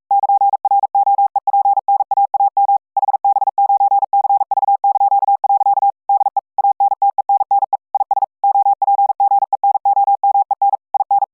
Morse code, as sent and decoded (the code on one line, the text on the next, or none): -.--. .-. --- . .--. -. .- .- -- / .... -... ----. -.-. ...- -.--.- .-.-.- / -.. / .- -. - . -. -. / .. ... / --- .--. --. -. --- -- . -. / .. -.
<KN>ROEPNAAM HB9CV). DE ANTENNE IS OPGENOMEN IN